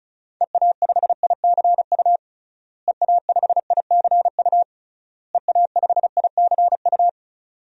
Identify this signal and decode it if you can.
Morse code, heard as a keyed tone, about 35 wpm, 700 Hz. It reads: EA5ICU EA5ICU EA5ICU
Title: DXCC-100 Call Signs - 6 Characters - Repeated 3x - 35wpm